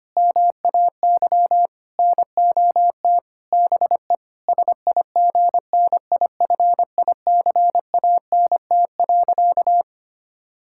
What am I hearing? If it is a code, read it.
Morse code, heard as a keyed tone, about 25 wpm, 700 Hz. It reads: MAY NOT BE SIGNIFICANT.
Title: Sentences from Top 1000 Words - Review Part 6 - 25wpm